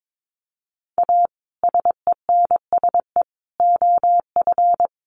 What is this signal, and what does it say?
Morse code, heard as a keyed tone, about 22 wpm, 700 Hz. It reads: A SENSE OF
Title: Sets of 3 Words 22wpm